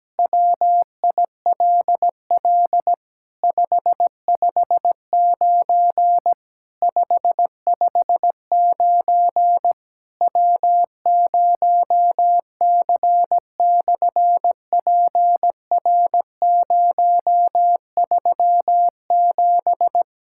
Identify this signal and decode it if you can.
Morse code, heard as a keyed tone, about 17 wpm, 700 Hz. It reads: WILL 559 559 W0C/PR037
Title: SOTA QSO Practice 17wpm